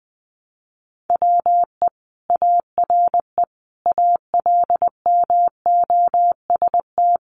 Morse code, heard as a keyed tone, about 20 wpm, 700 Hz. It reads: WE ARE ALMOST
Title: Sentences from Top 700 Words 20wpm